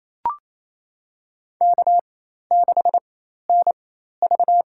Morse code, heard as a keyed tone, about 28 wpm, spaced as for 13 wpm, 700 Hz. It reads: K6NV
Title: USA Call Signs - 4 Characters - 4x Character Spacing - 28wpm